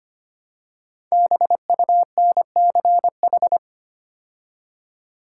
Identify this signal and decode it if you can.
Morse code, heard as a keyed tone, about 25 wpm, 700 Hz. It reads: BUNCH
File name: 5 Letter Words - First 3 Letters Given - 25wpm